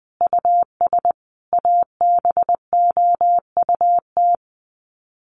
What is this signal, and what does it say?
Morse code, heard as a keyed tone, about 20 wpm, 700 Hz. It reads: US ABOUT
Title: Sets of 5 Words Encore2 20wpm